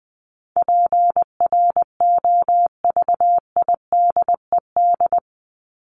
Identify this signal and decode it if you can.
Morse code, heard as a keyed tone, about 20 wpm, 700 Hz. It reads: PROVIDED
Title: Top 1751-2000 Words 20wpm